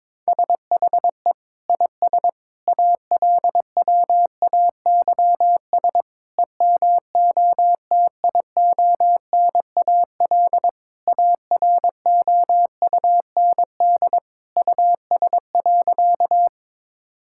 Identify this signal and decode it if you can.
Morse code, heard as a keyed tone, about 22 wpm, 700 Hz. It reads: SHE IS ALWAYS EMOTIONAL AROUND US.
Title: Sentences from Top 1000 Words - Review Part 5 - 22wpm